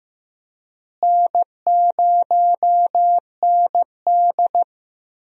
Morse code, heard as a keyed tone, about 16 wpm, 700 Hz. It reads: N0ND
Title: USA Call Signs - 4 Characters - Repeated 3x - 15wpm